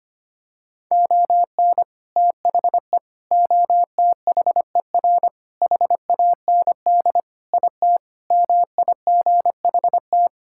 Morse code, heard as a keyed tone, about 25 wpm, 700 Hz. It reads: ON THE OTHER HAND IT MIGHT
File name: practice / 25wpm / Sentences with 3-Word Phrases - 25wpm